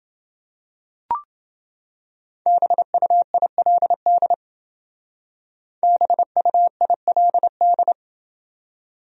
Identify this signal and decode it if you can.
Morse code, heard as a keyed tone, about 29 wpm, 700 Hz. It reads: BUILD BUILD